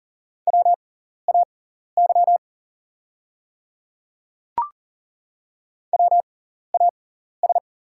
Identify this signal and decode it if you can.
Morse code, heard as a keyed tone, about 40 wpm, spaced as for 14 wpm, 690 Hz. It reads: WAY WAS
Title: 6x Character Spacing - Top 100 Words - 40wpm